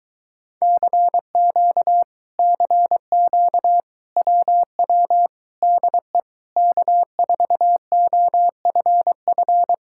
Morse code, heard as a keyed tone, about 23 wpm, 700 Hz. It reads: CQ CQ WW DE K4OFF